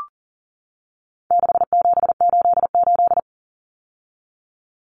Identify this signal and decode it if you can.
Morse code, heard as a keyed tone, about 40 wpm, 700 Hz. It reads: 6788
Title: Numbers - 4-Digit 40wpm